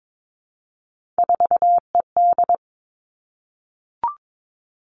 Morse code, heard as a keyed tone, about 22 wpm, 700 Hz. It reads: 4ED
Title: Three Challenging Letter-Number Combinations 22wpm